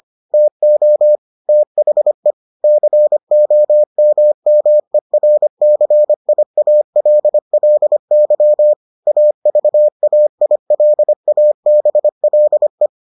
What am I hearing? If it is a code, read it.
Morse code, heard as a keyed tone, about 25 wpm, 600 Hz. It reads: TO THE COMMERCIALLY AVAILABLE